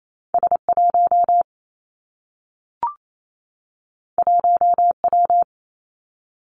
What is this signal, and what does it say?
Morse code, heard as a keyed tone, about 28 wpm, 700 Hz. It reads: S 1 1 W
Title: Two Challenging Letter-Number Combinations 28wpm